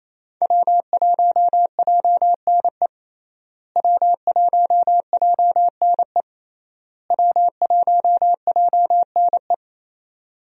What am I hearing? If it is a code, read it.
Morse code, heard as a keyed tone, about 28 wpm, 700 Hz. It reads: W1JNE W1JNE W1JNE
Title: USA Call Signs - 5 Characters - Repeated 3x - 28wpm